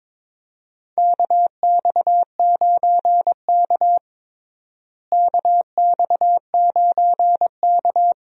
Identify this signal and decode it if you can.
Morse code, heard as a keyed tone, about 21 wpm, 700 Hz. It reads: KX9K KX9K
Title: USA Call Signs - 4 Characters - Repeated 2x - 22wpm